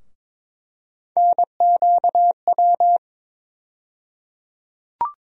Morse code, heard as a keyed tone, about 21 wpm, 700 Hz. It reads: NQW
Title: USA Call Sign Postfixes - 3 Letters 22wpm